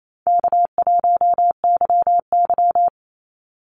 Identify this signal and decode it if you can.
Morse code, heard as a keyed tone, about 28 wpm, 700 Hz. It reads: K1YY
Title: Easy Call Signs Repeated 3x - 28wpm